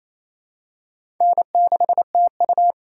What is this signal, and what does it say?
Morse code, heard as a keyed tone, about 27 wpm, 700 Hz. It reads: N6TU
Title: USA Call Signs - 4 Characters - 4x Character Spacing - 28wpm